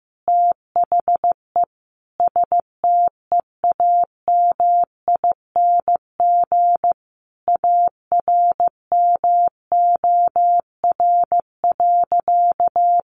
Morse code, heard as a keyed tone, about 14 wpm, 700 Hz. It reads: THE STEAMING ARMOR.
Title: A Sound Of Thunder 15wpm